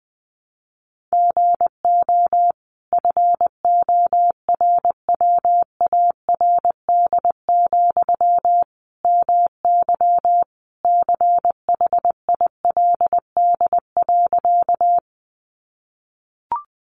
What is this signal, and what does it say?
Morse code, heard as a keyed tone, about 20 wpm, 700 Hz. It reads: GO FORWARD, MY CHILD.
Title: Sentences from Top 1300 Words 20wpm